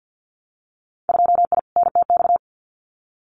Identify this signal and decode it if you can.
Morse code, heard as a keyed tone, about 50 wpm, 700 Hz. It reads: 2I NTX
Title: Winter Field Day Exchanges Encore 50wpm